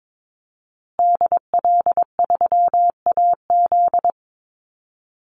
Morse code, heard as a keyed tone, about 22 wpm, 700 Hz. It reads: DL3AZ
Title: Call Signs Repeated 3x - 22wpm